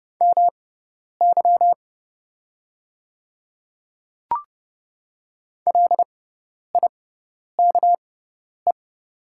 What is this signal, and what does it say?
Morse code, heard as a keyed tone, about 30 wpm, spaced as for 10 wpm, 700 Hz. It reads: MY LIKE
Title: 6x Character Spacing - Top 100 Words - 30wpm